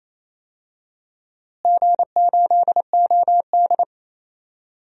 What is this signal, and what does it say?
Morse code, heard as a keyed tone, about 27 wpm, 700 Hz. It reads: G8OD
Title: DXCC-100 Call Signs - 4 Characters - Repeated 3x - 28wpm